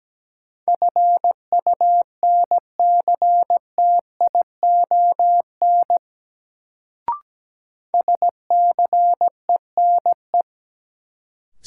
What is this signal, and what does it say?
Morse code, heard as a keyed tone, about 17 wpm, 700 Hz. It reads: FUNCTION SCENE
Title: Sentences from Top 1100 Words - New Word Drill - 17wpm